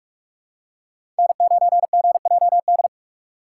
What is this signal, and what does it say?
Morse code, heard as a keyed tone, about 46 wpm, 700 Hz. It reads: N9GJD
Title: USA Call Signs - 5 Characters - 4x Character Spacing - 45wpm